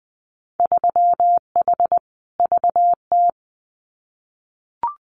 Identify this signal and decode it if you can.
Morse code, heard as a keyed tone, about 20 wpm, 700 Hz. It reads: 3H VT
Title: Winter Field Day Exchanges Encore 20wpm